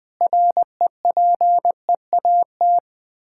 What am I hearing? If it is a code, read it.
Morse code, heard as a keyed tone, about 20 wpm, 700 Hz. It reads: REPEAT
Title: Sentences with 8-Word Phrases - 20wpm